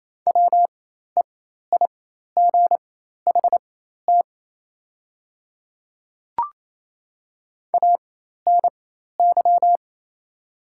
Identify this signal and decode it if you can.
Morse code, heard as a keyed tone, about 28 wpm, spaced as for 13 wpm, 690 Hz. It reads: WEIGHT ANY
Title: 4x Character Spacing - Top 500 Words - 28wpm